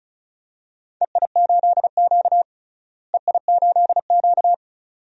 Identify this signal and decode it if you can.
Morse code, heard as a keyed tone, about 36 wpm, 700 Hz. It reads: EI8Q EI8Q
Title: DXCC-100 Call Signs - 4 Characters - Repeated 2x - 35wpm